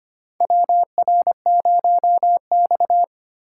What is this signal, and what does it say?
Morse code, heard as a keyed tone, about 25 wpm, 700 Hz. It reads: WR0X
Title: USA Call Signs - 4x Character Spacing - 25wpm